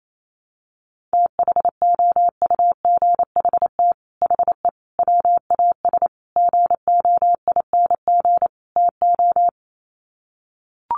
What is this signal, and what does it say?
Morse code, heard as a keyed tone, about 28 wpm, 700 Hz. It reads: THOUGHT HE WAS GOING TO
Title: Sets of 5 Words Encore2 28wpm